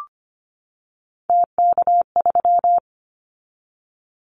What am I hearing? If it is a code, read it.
Morse code, heard as a keyed tone, about 25 wpm, 700 Hz. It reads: TK3
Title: Callsign Prefixes for Top 100 DXCC - 25wpm